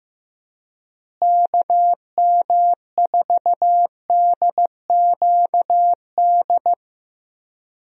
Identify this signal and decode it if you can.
Morse code, heard as a keyed tone, about 15 wpm, 700 Hz. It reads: KM4DQD
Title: USA Call Signs - 6 Characters - Repeated 3x - 15wpm